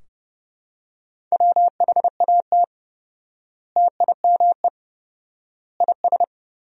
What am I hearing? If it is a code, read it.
Morse code, heard as a keyed tone, about 32 wpm, 700 Hz. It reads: WHAT TIME IS
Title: 4x Word-Spacing - Sentences from Top 100 Words - 30wpm